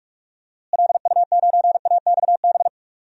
Morse code, heard as a keyed tone, about 46 wpm, 700 Hz. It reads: LU9AXB
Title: DXCC-100 Call Signs - 6 Characters - Repeated 2x - 45wpm